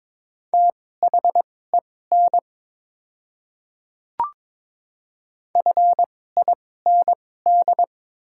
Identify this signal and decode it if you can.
Morse code, heard as a keyed tone, about 22 wpm, spaced as for 16 wpm, 700 Hz. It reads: THEN FIND
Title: 2x Character Spacing - Top 200 Words - 22wpm